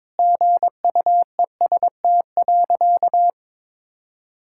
Morse code, heard as a keyed tone, about 22 wpm, 700 Hz. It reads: GUEST.
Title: Sentences from Top 1600 Words 22wpm